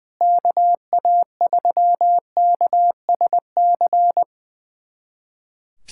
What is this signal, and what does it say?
Morse code, heard as a keyed tone, about 21 wpm, 700 Hz. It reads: KA3KSC
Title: USA Callsign DE Callsign 20wpm